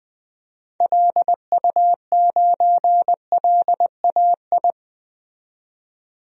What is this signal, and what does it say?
Morse code, heard as a keyed tone, about 20 wpm, 700 Hz. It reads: LU9LAI
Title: DXCC-100 Call Signs - 6 Characters - 20wpm